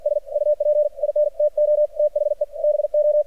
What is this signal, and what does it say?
Morse code, heard as a keyed tone, about 53 wpm, 600 Hz. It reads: S UP INTO THE BO
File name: archived / CRSnet